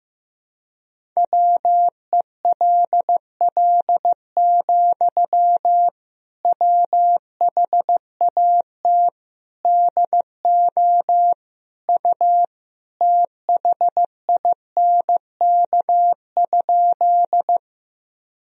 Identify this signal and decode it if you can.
Morse code, heard as a keyed tone, about 15 wpm, 700 Hz. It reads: WELL, WHAT DO U THINK?
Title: Sentences from Top 200 Words and Abbreviations 15wpm